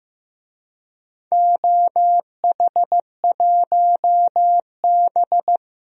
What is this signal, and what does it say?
Morse code, heard as a keyed tone, about 15 wpm, 700 Hz. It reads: OH1B
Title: DXCC-100 Call Signs - 4 Characters - 15wpm